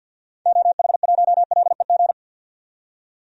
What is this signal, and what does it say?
Morse code, heard as a keyed tone, about 46 wpm, 700 Hz. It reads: OH1LEG
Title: DXCC-100 Call Signs - 6 Characters - 50wpm